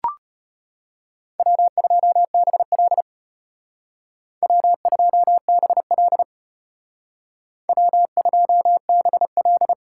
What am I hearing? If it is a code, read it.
Morse code, heard as a keyed tone, about 32 wpm, 700 Hz. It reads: W2BL W2BL W2BL